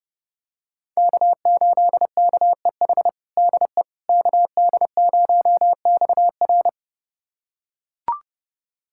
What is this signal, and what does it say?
Morse code, heard as a keyed tone, about 32 wpm, 700 Hz. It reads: K8KEH DE KD0XR
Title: USA Callsign DE Callsign - 5 Characters - 30wpm